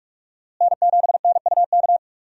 Morse code, heard as a keyed tone, about 46 wpm, 700 Hz. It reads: N7NUK
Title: USA Call Signs - 5 Characters - Repeated 3x - 45wpm